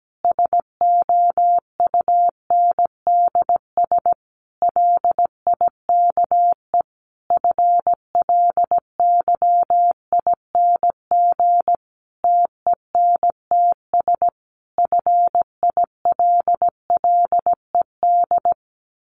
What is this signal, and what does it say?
Morse code, heard as a keyed tone, about 17 wpm, 700 Hz. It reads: SOUNDS LIKE FLYING TENTS FILLED
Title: A Sound Of Thunder 17wpm